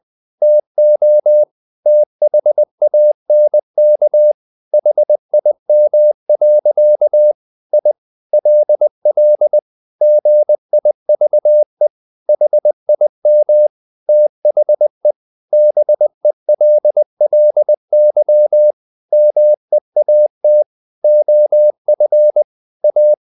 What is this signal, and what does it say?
Morse code, heard as a keyed tone, about 20 wpm, 600 Hz. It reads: TO THANK HIM. I LL GIVE HIM THE BELLY MEAT OF A
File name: the old man and the sea / the old man and the sea part 1 20wpm